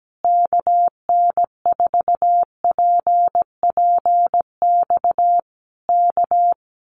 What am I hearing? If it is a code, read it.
Morse code, heard as a keyed tone, about 17 wpm, 700 Hz. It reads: KN4PPX K
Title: Common QSO Phrases Encore 17wpm